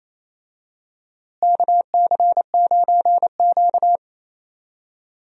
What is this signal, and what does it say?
Morse code, heard as a keyed tone, about 28 wpm, 700 Hz. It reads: KC9Q
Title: USA Call Signs - 4 Characters - Repeated 3x - 28wpm